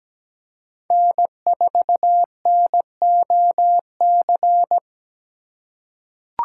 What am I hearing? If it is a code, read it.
Morse code, heard as a keyed tone, about 17 wpm, 700 Hz. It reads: N4NOC
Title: USA Call Signs - 5 Characters - 4x Character Spacing - 17wpm